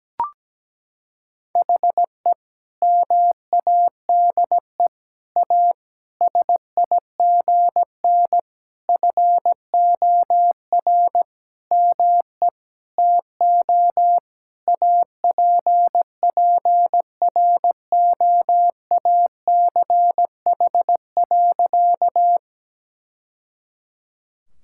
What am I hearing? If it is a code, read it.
Morse code, heard as a keyed tone, about 17 wpm, 700 Hz. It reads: HE MADE A SIGN FOR ME TO APPROACH.
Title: Sentences from Top 1000 Words - Review Part 6 - 17wpm